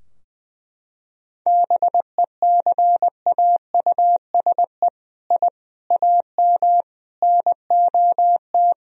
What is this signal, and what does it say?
Morse code, heard as a keyed tone, about 20 wpm, 700 Hz. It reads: BECAUSE I AM NOT
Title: Sentences from Top 1000 Words - Review Part 3 - 20wpm